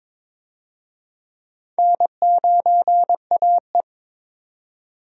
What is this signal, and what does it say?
Morse code, heard as a keyed tone, about 22 wpm, 700 Hz. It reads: N9AE
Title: USA Call Signs - 4 Characters - 22wpm